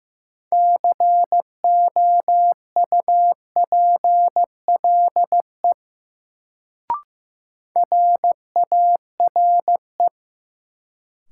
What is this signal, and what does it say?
Morse code, heard as a keyed tone, about 15 wpm, 700 Hz. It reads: COUPLE RARE
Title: Sentences from Top 1100 Words - New Word Drill Alt - 15wpm